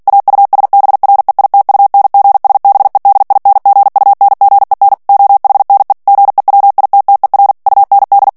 Morse code, heard as a keyed tone, about 48 wpm, 800 Hz. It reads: AUSBREITUNGSBEDINGUNGEN OHNE GEWITTER UND
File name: morse mp3 / ALLEMAND texte clair 2 vitesse 45 mots minute